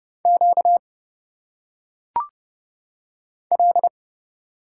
Morse code, heard as a keyed tone, about 29 wpm, 700 Hz. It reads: Q L
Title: Single Letters - Phonetic Pronunciation 30wpm